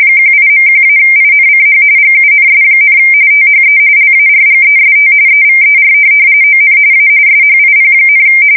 RTTY, about 45.45 baud. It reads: 18.1-INCH GUNS, THE LARGEST EVER ON A SHIP,
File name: fsk170 / USS-New-Jersey